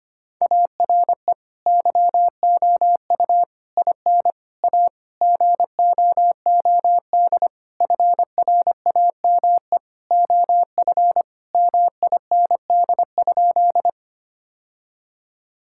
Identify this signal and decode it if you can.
Morse code, heard as a keyed tone, about 25 wpm, 700 Hz. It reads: ARE YOU IN A GOOD FRAME OF MIND?
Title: Sentences from Top 1000 Words - Review Part 6 - 25wpm